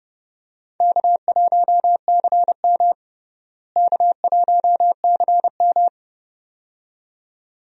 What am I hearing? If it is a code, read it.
Morse code, heard as a keyed tone, about 29 wpm, 700 Hz. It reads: K1CM K1CM
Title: USA Call Signs - 4 Characters - Repeated 2x - 30wpm